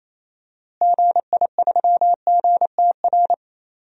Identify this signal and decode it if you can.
Morse code, heard as a keyed tone, about 28 wpm, 700 Hz. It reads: GI3GTR
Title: DXCC-100 Call Signs - 6 Characters - Repeated 2x - 28wpm